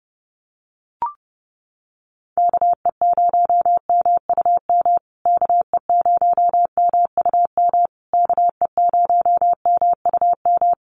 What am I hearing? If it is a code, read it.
Morse code, heard as a keyed tone, about 30 wpm, 700 Hz. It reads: KE0MUM KE0MUM KE0MUM
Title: USA Call Signs Repeated 3x 30wpm